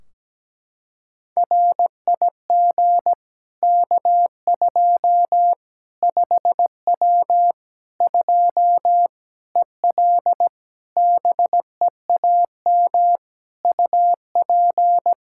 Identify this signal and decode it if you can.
Morse code, heard as a keyed tone, about 17 wpm, 700 Hz. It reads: RIG K2 5W 2 EL BEAM UP
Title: Practice QSO - Exchange 2 - Info - 17wpm